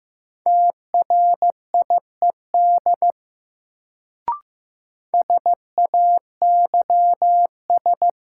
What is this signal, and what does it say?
Morse code, heard as a keyed tone, about 15 wpm, 700 Hz. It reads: TRIED SAYS
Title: Sets of 2 Words - New Word Drill - 15wpm